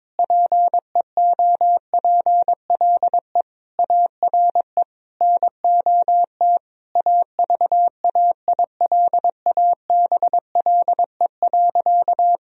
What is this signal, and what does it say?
Morse code, heard as a keyed tone, about 22 wpm, 700 Hz. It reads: PEOPLE ARE NOT AVAILABLE.
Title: Sentences from Top 1000 Words - Review Part 3 - 22wpm